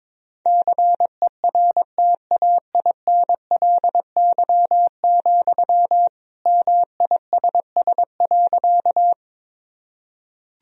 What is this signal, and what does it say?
Morse code, heard as a keyed tone, about 21 wpm, 700 Hz. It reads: CERTAINLY, MISS.
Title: Sentences from Top 1000 Words - Review Part 3 - 22wpm